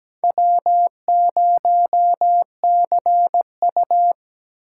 Morse code, heard as a keyed tone, about 17 wpm, 700 Hz. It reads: W0CU
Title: USA Call Signs - 4 Characters - 4x Character Spacing - 17wpm